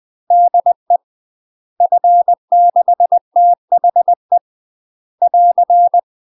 Morse code, heard as a keyed tone, about 20 wpm, 700 Hz. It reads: DE F6THE <AR>
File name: UK MASTER QSOs TRAINING / 20WPM / UK MASTER QSOs 0029